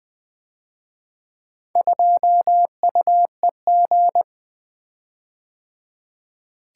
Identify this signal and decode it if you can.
Morse code, heard as a keyed tone, about 24 wpm, 700 Hz. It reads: 2UEG
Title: Speed-Racing - Four Letter-Number Combinations - 20wpm